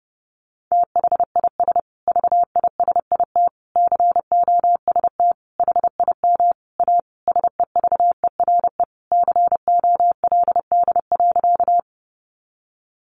Morse code, heard as a keyed tone, about 30 wpm, 700 Hz. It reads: THIS VISIT COST HIM A SEVERE COLD.
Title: Sentences from Top 1400 Words 30wpm